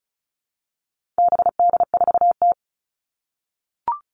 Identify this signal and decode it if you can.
Morse code, heard as a keyed tone, about 34 wpm, 700 Hz. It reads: BD4T
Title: All Cut Numbers - 4-Digit 35wpm